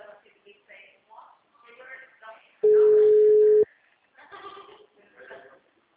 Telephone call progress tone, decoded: ringback tone

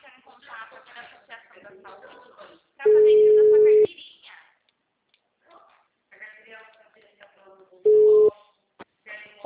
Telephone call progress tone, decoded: ringback tone

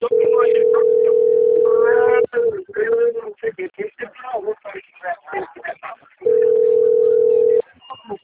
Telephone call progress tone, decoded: ringback tone

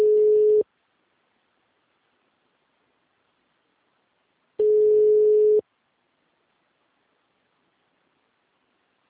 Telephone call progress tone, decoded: ringback tone

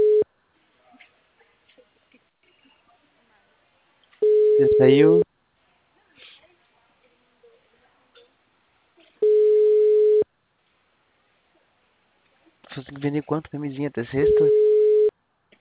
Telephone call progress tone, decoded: ringback tone

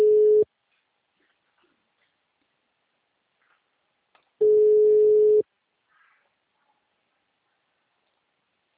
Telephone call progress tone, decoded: ringback tone

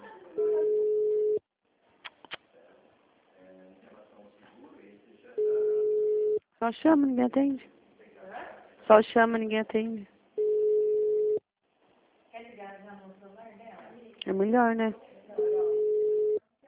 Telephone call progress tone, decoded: ringback tone

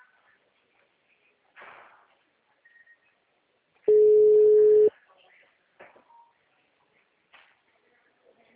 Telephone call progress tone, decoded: ringback tone

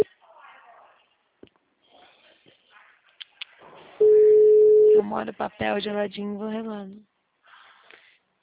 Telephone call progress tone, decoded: ringback tone

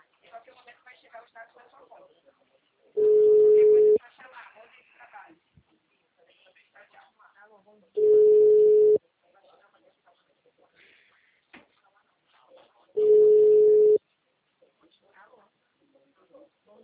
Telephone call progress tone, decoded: ringback tone

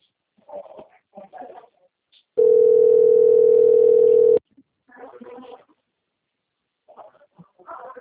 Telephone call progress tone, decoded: ringback tone

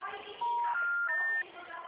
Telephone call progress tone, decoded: special information tone

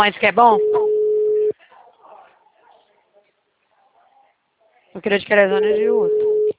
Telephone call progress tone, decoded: ringback tone

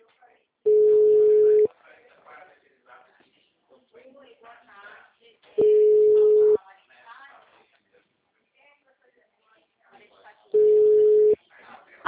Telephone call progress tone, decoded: ringback tone

